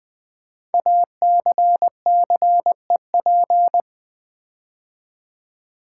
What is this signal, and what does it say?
Morse code, heard as a keyed tone, about 20 wpm, 700 Hz. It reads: ACCEP
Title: Build A Word - 5 Letters At A Time - 10-Letter Words - 20wpm